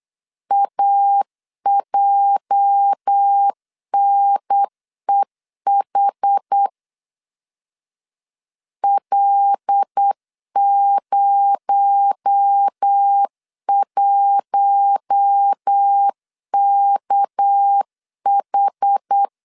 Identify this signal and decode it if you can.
Morse code, heard as a keyed tone, about 8 wpm, 800 Hz. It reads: AJNEH L01KH
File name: LEZIONI / 8 YJ45 / 8 YJ 8wpm 3313